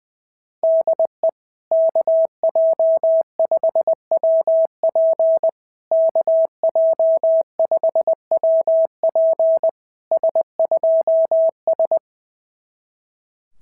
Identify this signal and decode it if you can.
Morse code, heard as a keyed tone, about 20 wpm, 650 Hz. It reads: DE KJ5WP KJ5WP S2S
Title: SOTA QSO Practice 20wpm